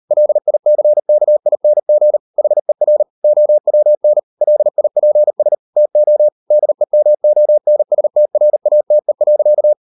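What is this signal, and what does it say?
Morse code, heard as a keyed tone, about 38 wpm, 600 Hz. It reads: LICKING HER OWN LIPS TO DEMONSTRATE.
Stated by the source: Morse code practice at 38 WPM, sending Ken Follett's novel Never in English, from CW med Gnister.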